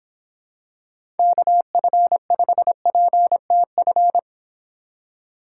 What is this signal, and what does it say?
Morse code, heard as a keyed tone, about 26 wpm, 700 Hz. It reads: KF5PTF